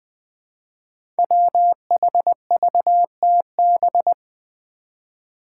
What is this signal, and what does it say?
Morse code, heard as a keyed tone, about 20 wpm, 700 Hz. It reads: WHVTB
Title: Five Challenging Letter-Number Combinations 20wpm